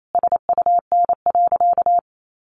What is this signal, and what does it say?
Morse code, heard as a keyed tone, about 28 wpm, 700 Hz. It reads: SUN.
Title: Sentences from Top 300 Words 28wpm